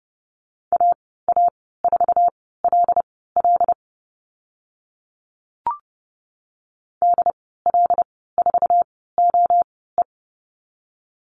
Morse code, heard as a keyed tone, about 30 wpm, spaced as for 17 wpm, 700 Hz. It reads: AA4LL DL4OE
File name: Lesson 15g - Number 5 - Call Signs 30wpm